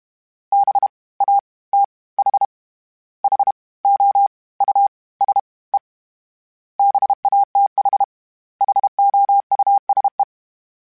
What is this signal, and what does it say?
Morse code, heard as a keyed tone, about 32 wpm, 800 Hz. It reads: B A T H H O U S E BATH HOUSE
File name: General / General-mp3s / Houses @30